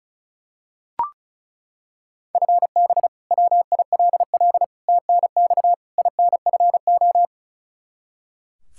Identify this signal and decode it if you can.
Morse code, heard as a keyed tone, about 36 wpm, 700 Hz. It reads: FB WILL TNX INFO
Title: Practice QSO - Exchange 3 - More Info - 35wpm